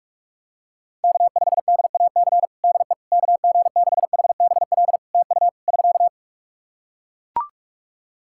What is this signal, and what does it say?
Morse code, heard as a keyed tone, about 42 wpm, 700 Hz. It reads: K5DAC DE KG6HBL TU <SK>
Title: K1USN SST QSO Practice 45wpm